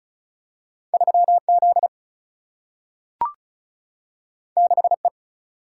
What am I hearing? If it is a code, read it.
Morse code, heard as a keyed tone, about 36 wpm, 700 Hz. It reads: 3Z 6E
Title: Single-Letter Single-Number 35wpm